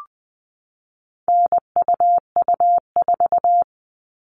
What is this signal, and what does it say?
Morse code, heard as a keyed tone, about 20 wpm, 700 Hz. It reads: NUU4
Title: All Cut Numbers - 4-Digit 20wpm